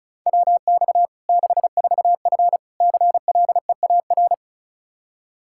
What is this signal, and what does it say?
Morse code, heard as a keyed tone, about 36 wpm, 700 Hz. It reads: WX 64F CLEAR
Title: K1USN SST QSO Practice 35wpm